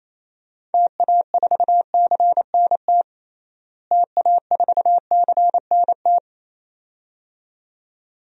Morse code, heard as a keyed tone, about 28 wpm, 700 Hz. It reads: TA4CNT TA4CNT
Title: DXCC-100 Call Signs - 6 Characters - Repeated 2x - 28wpm